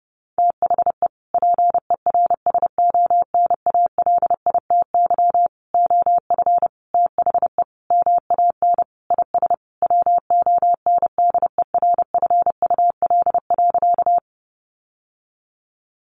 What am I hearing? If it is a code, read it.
Morse code, heard as a keyed tone, about 28 wpm, 700 Hz. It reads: THE PERSONALITY OF THE MAN IS WONDERFUL.
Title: Sentences from Top 1400 Words 30wpm